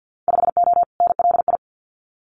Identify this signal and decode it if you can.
Morse code, heard as a keyed tone, about 48 wpm, 700 Hz. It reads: 5O NLI
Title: Winter Field Day Exchanges 50wpm